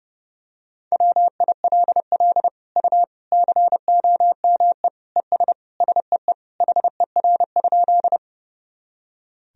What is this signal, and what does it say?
Morse code, heard as a keyed tone, about 32 wpm, 700 Hz. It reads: WILL U COME ES SEE HER?
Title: Sentences from Top 100 Words and Abbreviations 30wpm